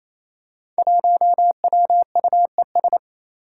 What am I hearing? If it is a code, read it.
Morse code, heard as a keyed tone, about 28 wpm, 700 Hz. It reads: 1WUES
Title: Five Challenging Letter-Number Combinations 28wpm